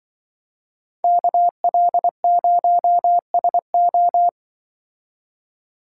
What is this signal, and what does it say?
Morse code, heard as a keyed tone, about 24 wpm, 700 Hz. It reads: KL0SO
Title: Speed Racing - USA 2x2 Call Signs 20wpm